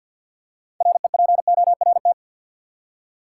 Morse code, heard as a keyed tone, about 52 wpm, 700 Hz. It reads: REPORT
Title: Six Letter Words - 50wpm